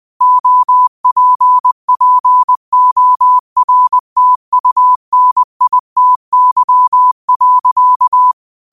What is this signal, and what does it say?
Morse code, heard as a keyed tone, about 20 wpm, 1000 Hz. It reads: OPPORTUNITY.